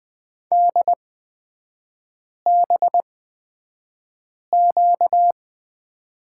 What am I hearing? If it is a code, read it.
Morse code, heard as a keyed tone, about 20 wpm, spaced as for 5 wpm, 700 Hz. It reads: DBQ
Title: 236 Lesson 38e - Letter q - Three Characters 20wpm